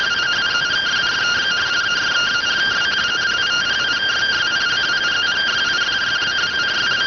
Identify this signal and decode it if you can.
Two-tone teleprinter signal, about 170 Hz shift.